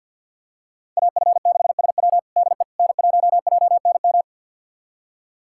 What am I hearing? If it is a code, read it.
Morse code, heard as a keyed tone, about 49 wpm, 700 Hz. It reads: AF6SW DE N1JNM
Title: USA Callsign DE Callsign - 5 Characters - 50wpm